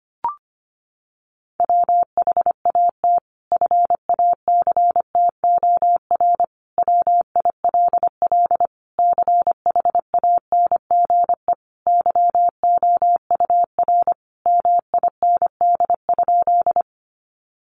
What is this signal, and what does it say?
Morse code, heard as a keyed tone, about 25 wpm, 700 Hz. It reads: WHAT FACTOR WILL CHANGE YOUR MIND?
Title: Sentences from Top 1300 Words 25wpm